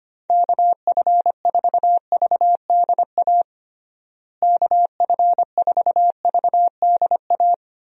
Morse code, heard as a keyed tone, about 25 wpm, 700 Hz. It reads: KF4VDA KF4VDA
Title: USA Call Signs - 6 Characters - Repeated 2x - 25wpm